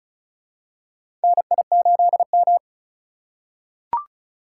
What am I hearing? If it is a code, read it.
Morse code, heard as a keyed tone, about 34 wpm, 700 Hz. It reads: NI8M
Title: USA Call Signs - 4 Characters - 4x Character Spacing - 35wpm